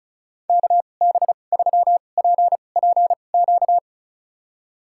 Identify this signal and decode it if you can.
Morse code, heard as a keyed tone, about 34 wpm, spaced as for 25 wpm, 700 Hz. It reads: KB3PPQ
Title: USA Call Signs - 2x Character Spacing - 35wpm